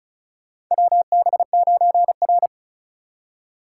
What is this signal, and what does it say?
Morse code, heard as a keyed tone, about 34 wpm, 700 Hz. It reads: WB9R
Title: USA Call Signs - 4 Characters - 6x Character Spacing - 35wpm